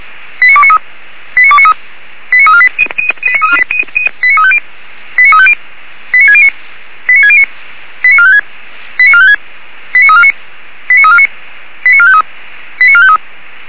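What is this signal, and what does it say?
Five-tone selective call (ZVEI), heard as quick runs of five tones, tones of about 70 ms. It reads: 89293 89293 89348 89350 89350 89609 89609 89456 89456 89349 89349 89453 89453
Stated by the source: ZVEI five-tone call received on an SDR.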